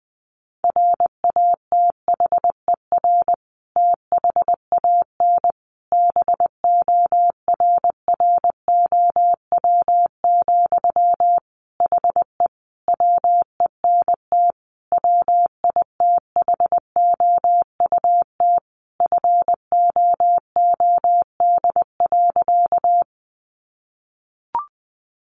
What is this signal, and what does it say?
Morse code, heard as a keyed tone, about 20 wpm, 700 Hz. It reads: RATHER THAN BORROW, HE WENT WITHOUT FOOD.
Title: Sentences from Top 1500 Words 20wpm